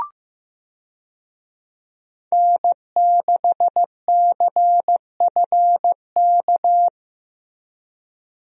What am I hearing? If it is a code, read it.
Morse code, heard as a keyed tone, about 15 wpm, 700 Hz. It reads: N6CFK
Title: USA Call Signs - 5 Characters - Repeated 3x - 15wpm